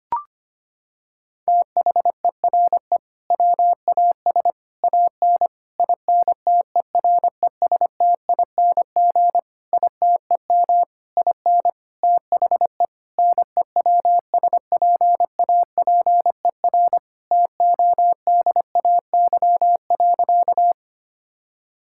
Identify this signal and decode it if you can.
Morse code, heard as a keyed tone, about 25 wpm, 700 Hz. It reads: THERE WAS AN INTERESTING ITEM IN THE NEWSPAPER TODAY.
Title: Sentences from Top 1000 Words - Review Part 6 - 25wpm